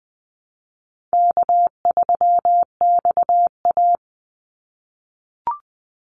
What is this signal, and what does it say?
Morse code, heard as a keyed tone, about 19 wpm, 700 Hz. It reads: K3XA
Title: Easy Call Signs - 20wpm